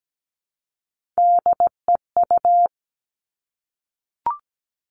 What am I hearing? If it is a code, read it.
Morse code, heard as a keyed tone, about 17 wpm, 700 Hz. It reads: DEU
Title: Three Challenging Letter-Number Combinations 17wpm